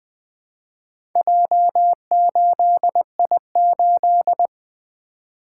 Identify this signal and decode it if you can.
Morse code, heard as a keyed tone, about 20 wpm, 700 Hz. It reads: J8I8
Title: Four Letter-Number Combinations - 20wpm